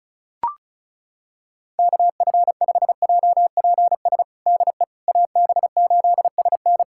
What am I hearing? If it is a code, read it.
Morse code, heard as a keyed tone, about 36 wpm, 700 Hz. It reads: KF5JPS DE AB8SN
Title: USA Callsign DE Callsign 35wpm